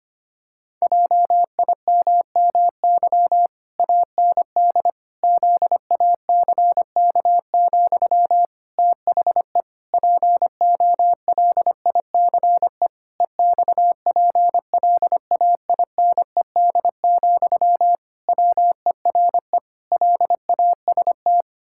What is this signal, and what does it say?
Morse code, heard as a keyed tone, about 25 wpm, 700 Hz. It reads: JIMMY AND ZACK, THE POLICE EXPLAINED, WERE LAST
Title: Pangram Challenge v2 - 25wpm